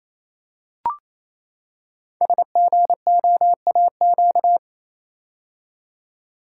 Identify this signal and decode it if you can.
Morse code, heard as a keyed tone, about 28 wpm, 700 Hz. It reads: SGOAQ
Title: Code Groups - 5 Letters - 28wpm